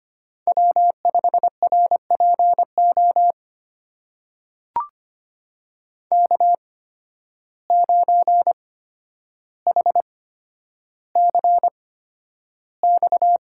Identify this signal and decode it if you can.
Morse code, heard as a keyed tone, about 25 wpm, spaced as for 11 wpm, 700 Hz. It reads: W5RPO K9HCX
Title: USA Call Signs - 5 Characters - 8x Character Spacing - 25wpm